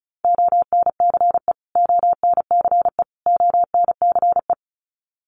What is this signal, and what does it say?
Morse code, heard as a keyed tone, about 34 wpm, 700 Hz. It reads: ONCE ONCE ONCE
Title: Top 500 Words Repeated 35wpm